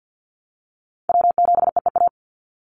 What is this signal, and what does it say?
Morse code, heard as a keyed tone, about 50 wpm, 700 Hz. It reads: W7EEA
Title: USA Call Signs 50wpm